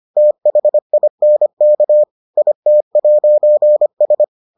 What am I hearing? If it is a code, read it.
Morse code, heard as a keyed tone, about 25 wpm, 600 Hz. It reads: THINK IT'S